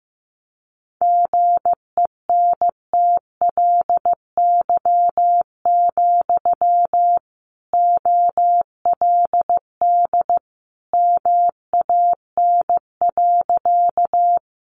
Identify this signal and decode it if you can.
Morse code, heard as a keyed tone, about 15 wpm, 700 Hz. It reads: GENTLY, OLD MAN.
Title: Sentences from Top 1500 Words 15wpm